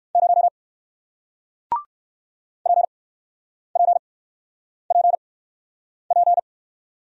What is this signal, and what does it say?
Morse code, heard as a keyed tone, about 60 wpm, spaced as for 9 wpm, 700 Hz. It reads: 0 PPPP